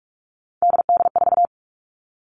Morse code, heard as a keyed tone, about 44 wpm, 700 Hz. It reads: DD4
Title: All Cut Numbers - 3-Digit 45wpm